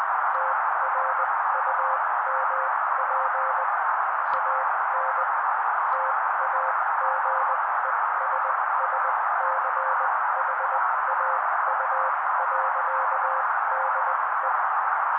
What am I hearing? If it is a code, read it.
Morse code, heard as a keyed tone, about 20 wpm, spaced as for 16 wpm, 530 Hz. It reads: TRUMP AN TAGESSCHAU.DE
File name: daily / 16wpm noise